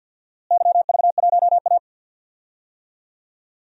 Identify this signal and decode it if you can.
Morse code, heard as a keyed tone, about 52 wpm, 700 Hz. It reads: YV1A
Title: DXCC-100 Call Signs - 4 Characters - Repeated 2x - 50wpm